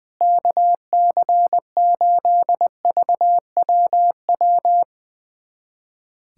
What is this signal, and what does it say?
Morse code, heard as a keyed tone, about 20 wpm, 700 Hz. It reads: KC8VWW
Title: USA Callsign DE Callsign - 6 Characters - 20wpm